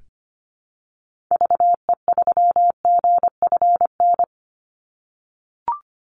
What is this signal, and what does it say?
Morse code, heard as a keyed tone, about 25 wpm, 700 Hz. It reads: VE3GFN
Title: Call Signs - 25wpm